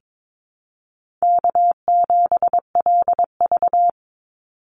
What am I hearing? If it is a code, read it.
Morse code, heard as a keyed tone, about 22 wpm, 700 Hz. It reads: K7LV
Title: Easy Call Signs Repeated 2x - 22wpm